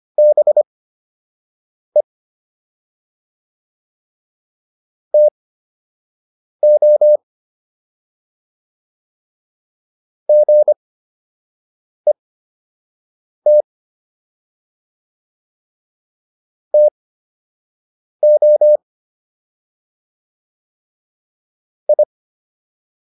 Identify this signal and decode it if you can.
Morse code, heard as a keyed tone, about 25 wpm, spaced as for 6 wpm, 600 Hz. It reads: BE TO GET TO I